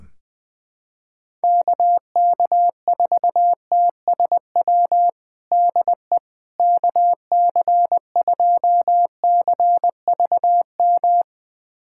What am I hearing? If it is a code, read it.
Morse code, heard as a keyed tone, about 21 wpm, 700 Hz. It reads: KK4TSW DE KC2CVM
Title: USA Callsign DE Callsign - 6 Characters - 20wpm